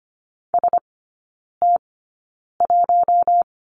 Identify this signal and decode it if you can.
Morse code, heard as a keyed tone, about 25 wpm, spaced as for 9 wpm, 700 Hz. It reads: ST1
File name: Lesson 17e - Letter u - Three Characters 25wpm